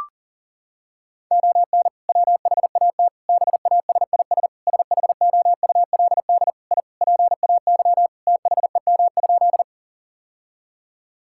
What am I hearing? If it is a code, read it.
Morse code, heard as a keyed tone, about 40 wpm, 700 Hz. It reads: ON WHAT BASIS SHOULD I PAY THEM?
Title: Sentences from Top 1000 Words - Review Part 5 - 40wpm